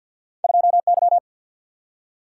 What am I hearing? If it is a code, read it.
Morse code, heard as a keyed tone, about 51 wpm, 700 Hz. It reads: 2Y
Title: Single-Letter Single-Number 50wpm